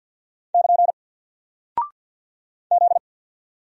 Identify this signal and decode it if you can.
Morse code, heard as a keyed tone, about 50 wpm, 700 Hz. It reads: <KN> Z
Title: Single Character (Letter:Number:Punctuation:Prosign) 50wpm